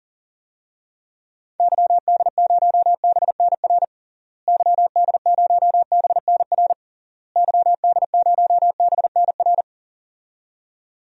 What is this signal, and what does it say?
Morse code, heard as a keyed tone, about 40 wpm, 700 Hz. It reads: YD0BNR YD0BNR YD0BNR
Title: DXCC-100 Call Signs - 6 Characters - Repeated 3x - 40wpm